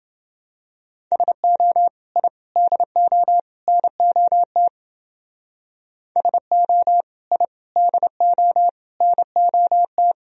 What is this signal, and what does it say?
Morse code, heard as a keyed tone, about 29 wpm, 700 Hz. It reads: SO I DO NOT SO I DO NOT